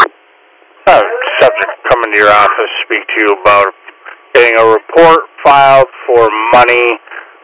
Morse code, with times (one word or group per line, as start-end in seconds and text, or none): none